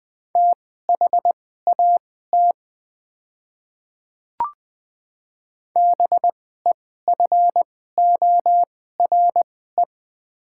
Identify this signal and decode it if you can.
Morse code, heard as a keyed tone, about 20 wpm, spaced as for 14 wpm, 700 Hz. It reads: THAT BEFORE
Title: 2x Character Spacing - Top 200 Words - 20wpm